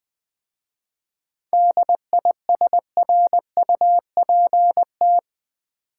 Morse code, heard as a keyed tone, about 20 wpm, 700 Hz. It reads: DISRUPT
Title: 7 Letter Words - First 3 Letters Given - 20wpm